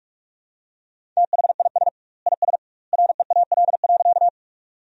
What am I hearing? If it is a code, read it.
Morse code, heard as a keyed tone, about 46 wpm, 700 Hz. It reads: THIS IS REAL.
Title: 2x Word-Spacing - Sentences from Top 300 Words - 45wpm